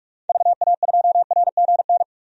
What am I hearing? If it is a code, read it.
Morse code, heard as a keyed tone, about 46 wpm, 700 Hz. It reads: VA2RGN
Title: DXCC-100 Call Signs - 6 Characters - 45wpm